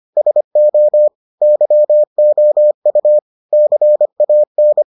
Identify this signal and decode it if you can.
Morse code, heard as a keyed tone, about 25 wpm, 600 Hz. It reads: SO YOU CAN